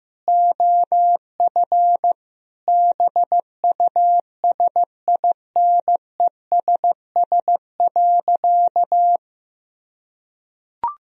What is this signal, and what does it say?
Morse code, heard as a keyed tone, about 16 wpm, 700 Hz. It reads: OF BUSINESS.
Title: Sentences from Top 1000 Words - Review Part 4 - 15wpm